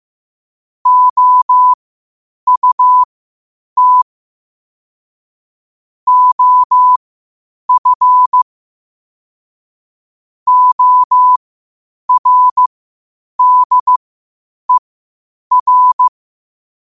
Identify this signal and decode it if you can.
Morse code, heard as a keyed tone, about 15 wpm, spaced as for 8 wpm, 1000 Hz. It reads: OUT OF ORDER
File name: qotdfiles / 8wpm